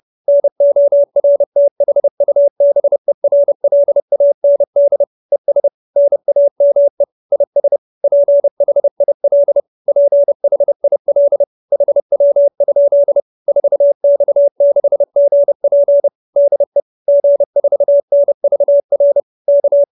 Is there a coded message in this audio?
Yes: Morse code from 0.3 to 19.8 s, heard as a keyed tone, about 30 wpm, 570 Hz. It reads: NORTHUBERLAND ES NAME IS PHIL PHIL HW? 4X6GP DE G4NVR K